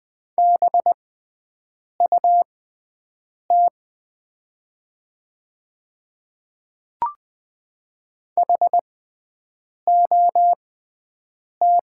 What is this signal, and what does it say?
Morse code, heard as a keyed tone, about 20 wpm, spaced as for 7 wpm, 700 Hz. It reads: BUT HOT
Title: 6x Character Spacing - Top 200 Words - 20wpm